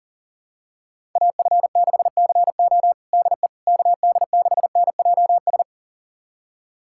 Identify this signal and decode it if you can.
Morse code, heard as a keyed tone, about 40 wpm, 700 Hz. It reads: AF6CO DE KD6NJS
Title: USA Callsign DE Callsign 40wpm